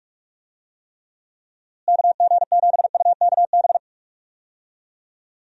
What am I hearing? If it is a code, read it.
Morse code, heard as a keyed tone, about 46 wpm, 700 Hz. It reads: KG7UKB
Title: USA Call Signs - 6 Characters - Repeated 3x - 45wpm